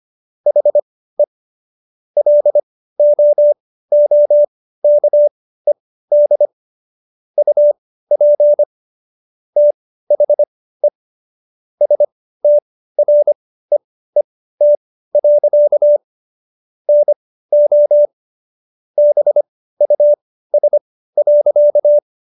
Morse code, heard as a keyed tone, about 25 wpm, spaced as for 15 wpm, 590 Hz. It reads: HE LOOKED UP THE STREET. NO BUS.